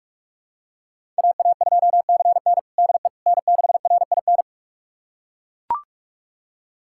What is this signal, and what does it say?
Morse code, heard as a keyed tone, about 42 wpm, 700 Hz. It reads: AA2CN DE N6RIN
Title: USA Callsign DE Callsign - 5 Characters - 45wpm